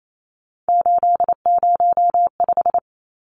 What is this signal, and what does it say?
Morse code, heard as a keyed tone, about 28 wpm, 700 Hz. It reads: 805
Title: Numbers - 3-Digit 28wpm